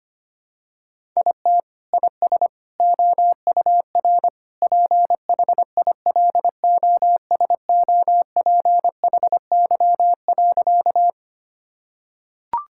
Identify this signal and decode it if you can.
Morse code, heard as a keyed tone, about 25 wpm, 700 Hz. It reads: IT IS OUR PHILOSOPHY.
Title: Sentences from Top 1000 Words - Review Part 6 - 25wpm